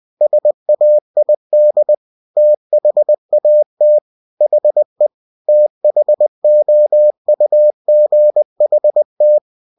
Morse code, heard as a keyed tone, about 20 wpm, 600 Hz. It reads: SAID THAT HE THOUGHT
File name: cw-books / winnie-the-pooh / winnie-20wpm-page-0061